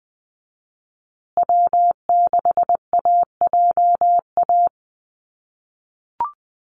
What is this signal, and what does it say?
Morse code, heard as a keyed tone, about 20 wpm, 700 Hz. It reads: W6AJA
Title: USA Call Signs Repeated 2x 20wpm